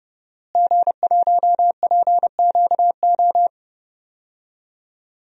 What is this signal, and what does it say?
Morse code, heard as a keyed tone, about 30 wpm, 700 Hz. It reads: G1PQO
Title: DXCC-100 Call Signs - 5 Characters - Repeated 3x - 30wpm